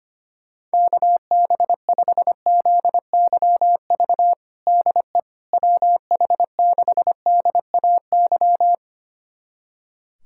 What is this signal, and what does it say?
Morse code, heard as a keyed tone, about 25 wpm, 700 Hz. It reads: KB5ZYV DE WH6DAY
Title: USA Callsign DE Callsign - 6 Characters - 25wpm